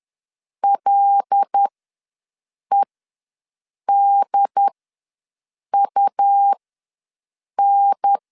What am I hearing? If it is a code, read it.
Morse code, heard as a keyed tone, about 10 wpm, spaced as for 6 wpm, 800 Hz. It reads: LEDUN
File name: LEZIONI / 5 LBVC / 5 LB ws 10wpm 3616